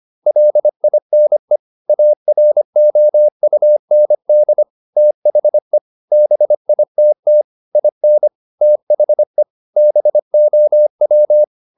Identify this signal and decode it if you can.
Morse code, heard as a keyed tone, about 25 wpm, 600 Hz. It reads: LINE AROUND THE BITT IN THE BOW